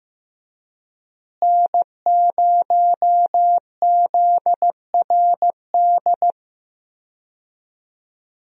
Morse code, heard as a keyed tone, about 15 wpm, 700 Hz. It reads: N0ZRD
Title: USA Call Signs - 5 Characters - Repeated 2x - 15wpm